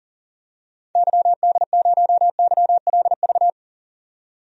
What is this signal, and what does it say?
Morse code, heard as a keyed tone, about 40 wpm, 700 Hz. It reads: YD0YLV
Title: DXCC-100 Call Signs - 6 Characters - 40wpm